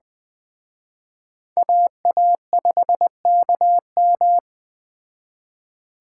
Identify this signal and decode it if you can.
Morse code, heard as a keyed tone, about 21 wpm, 700 Hz. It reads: AA5KM